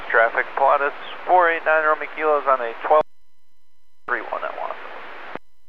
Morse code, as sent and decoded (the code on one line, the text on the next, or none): none